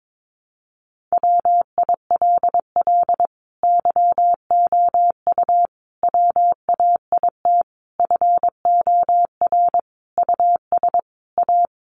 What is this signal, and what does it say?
Morse code, heard as a keyed tone, about 22 wpm, 700 Hz. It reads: WILL YOU WAIT FOR US A
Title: Sentences from Top 900 Words 22wpm